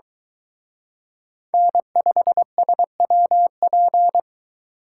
Morse code, heard as a keyed tone, about 23 wpm, 700 Hz. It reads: N5SWP